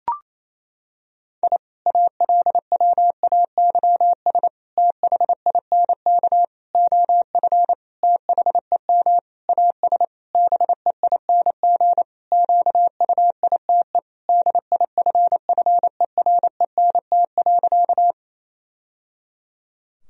Morse code, heard as a keyed tone, about 28 wpm, 700 Hz. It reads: I ALWAYS THINK OF THEM AS BEING QUITE DIFFERENT.
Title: Sentences from Top 1000 Words - Review Part 3 - 28wpm